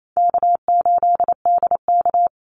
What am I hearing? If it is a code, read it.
Morse code, heard as a keyed tone, about 28 wpm, 700 Hz. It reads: K8DK
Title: Easy Call Signs - 28wpm